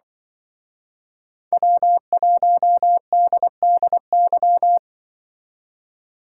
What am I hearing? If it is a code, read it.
Morse code, heard as a keyed tone, about 24 wpm, 700 Hz. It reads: W1DDY
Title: Speed Racing - USA 1x3 Call Signs 20wpm